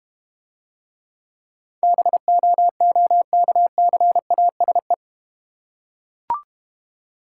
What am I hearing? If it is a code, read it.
Morse code, heard as a keyed tone, about 32 wpm, 700 Hz. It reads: BOOKCASE